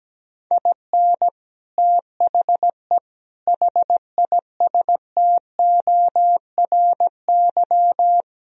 Morse code, heard as a keyed tone, about 17 wpm, 700 Hz. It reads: IN THE HISTORY
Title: Speed-Racing - Sets of 4 Words 17wpm